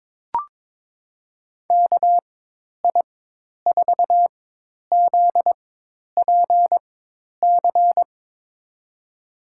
Morse code, heard as a keyed tone, about 22 wpm, spaced as for 10 wpm, 700 Hz. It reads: KI4ZPC
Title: USA Call Signs - 6 Characters - 4x Character Spacing - 22wpm